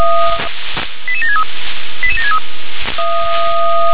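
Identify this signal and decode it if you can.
Five-tone selective call (ZVEI), heard as quick runs of five tones, tones of about 80 ms. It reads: 88763 88763
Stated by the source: ZVEI five-tone call received by SDR.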